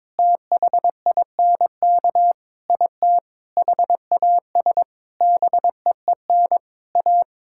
Morse code, heard as a keyed tone, about 22 wpm, 700 Hz. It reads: THINK IT HAS BEEN A
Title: Sentences from Top 1000 Words - Review Part 6 - 22wpm